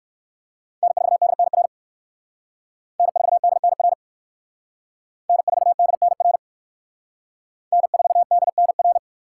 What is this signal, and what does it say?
Morse code, heard as a keyed tone, about 56 wpm, 700 Hz. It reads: N4DNR N4DNR N4DNR N4DNR
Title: Speed Racing - USA 1x3 Call Signs 45wpm